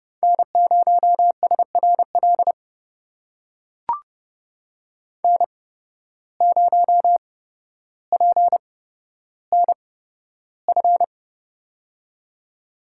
Morse code, heard as a keyed tone, about 30 wpm, spaced as for 13 wpm, 700 Hz. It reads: N0SRL N0PNF
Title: USA Call Signs - 5 Characters - 8x Character Spacing - 30wpm